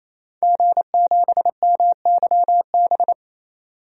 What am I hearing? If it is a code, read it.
Morse code, heard as a keyed tone, about 28 wpm, 700 Hz. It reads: G7MYB
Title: DXCC-100 Call Signs - 5 Characters - 28wpm